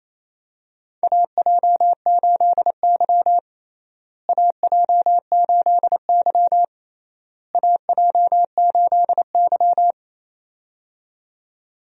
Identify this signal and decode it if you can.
Morse code, heard as a keyed tone, about 28 wpm, 700 Hz. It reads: AJ8Y AJ8Y AJ8Y
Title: USA Call Signs - 4 Characters - Repeated 3x - 28wpm